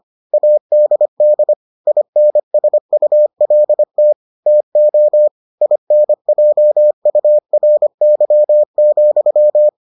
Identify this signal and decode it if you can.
Morse code, heard as a keyed tone, about 25 wpm, 600 Hz. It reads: ADD INSULT TO INJURY,